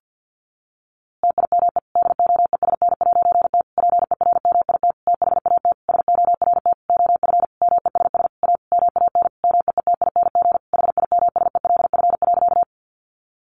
Morse code, heard as a keyed tone, about 50 wpm, 700 Hz. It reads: TIME DOESN'T PERMIT THAT SORT OF MESS A MAN MEETING HIMSELF.
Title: A Sound Of Thunder 50wpm